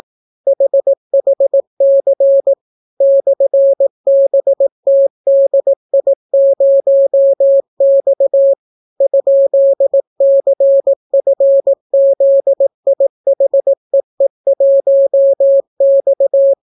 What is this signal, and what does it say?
Morse code, heard as a keyed tone, about 18 wpm, 550 Hz. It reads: HHC /BTDI0X ?CFZIHEE1X